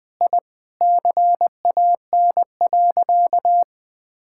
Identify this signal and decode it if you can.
Morse code, heard as a keyed tone, about 20 wpm, 700 Hz. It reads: I CAN.
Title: Sentences from Top 1000 Words - Review Part 6 - 20wpm